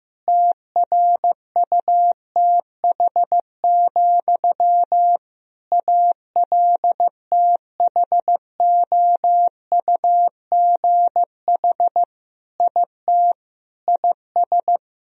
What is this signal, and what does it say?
Morse code, heard as a keyed tone, about 15 wpm, 700 Hz. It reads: TRUTH, ALTHOUGH IT IS
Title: Sentences from Top 1000 Words - Review Part 3 - 15wpm